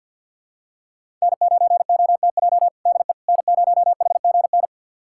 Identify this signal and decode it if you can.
Morse code, heard as a keyed tone, about 52 wpm, 700 Hz. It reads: N9OTJ DE N0HGN
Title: USA Callsign DE Callsign - 5 Characters - 50wpm